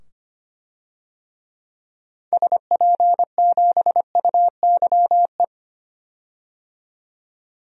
Morse code, heard as a keyed tone, about 25 wpm, 700 Hz. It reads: SP7UYE
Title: DXCC-100 Call Signs - 6 Characters - 25wpm